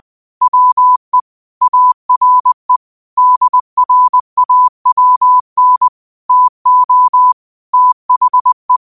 Morse code, heard as a keyed tone, about 20 wpm, 1000 Hz. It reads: WE ARE DRAWN TO THE